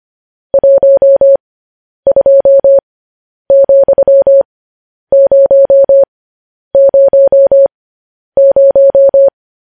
Morse code, heard as a keyed tone, about 25 wpm, spaced as for 10 wpm, 550 Hz. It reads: 12,000